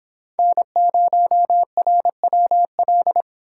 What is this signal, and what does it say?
Morse code, heard as a keyed tone, about 25 wpm, 700 Hz. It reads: N0RWL